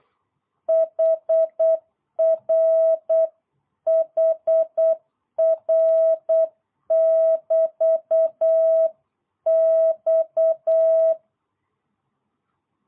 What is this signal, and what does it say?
Morse code, heard as a keyed tone, about 8 wpm, 650 Hz. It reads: HR HR <BT> X